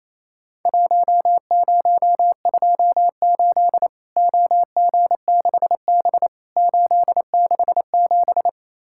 Morse code, heard as a keyed tone, about 28 wpm, 700 Hz. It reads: 1028 OG6B 867
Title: Worked All Europe DX Contest - QTC Exchange - 28wpm